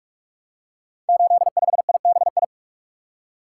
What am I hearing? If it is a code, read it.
Morse code, heard as a keyed tone, about 44 wpm, 700 Hz. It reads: 85IBI
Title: Code Groups - 5 Letter:Numbers - 45wpm